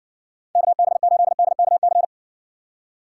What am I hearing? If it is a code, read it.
Morse code, heard as a keyed tone, about 60 wpm, 700 Hz. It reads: KB8DKC